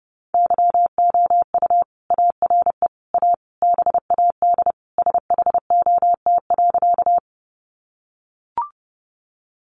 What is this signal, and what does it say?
Morse code, heard as a keyed tone, about 30 wpm, 700 Hz. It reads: YOU ARE A BAD SHOT.
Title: Sentences from Top 1100 Words 30wpm